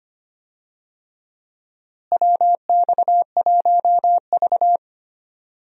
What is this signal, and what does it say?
Morse code, heard as a keyed tone, about 25 wpm, 700 Hz. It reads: WX1V
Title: USA Call Signs - 4 Characters - Repeated 3x - 25wpm